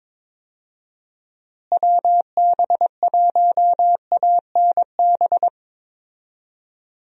Morse code, heard as a keyed tone, about 22 wpm, 700 Hz. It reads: WB1ANB
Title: USA Call Signs - 6 Characters - Repeated 3x - 22wpm